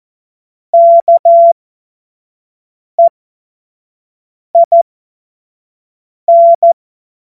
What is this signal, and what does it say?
Morse code, heard as a keyed tone, about 14 wpm, spaced as for 5 wpm, 680 Hz. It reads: KEIN
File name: Morsekurs / Witze / Witz 29. November 2022 25 F70